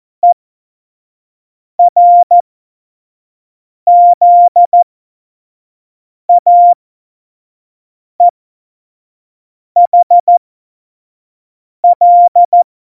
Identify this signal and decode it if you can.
Morse code, heard as a keyed tone, about 14 wpm, spaced as for 5 wpm, 700 Hz. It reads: ERZAEHL
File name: Morsekurs / Witze / Witz 18 August 2020, 25 F70